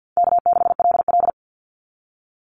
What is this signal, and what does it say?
Morse code, heard as a keyed tone, about 50 wpm, 700 Hz. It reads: K6LL
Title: Easy Call Signs - 50wpm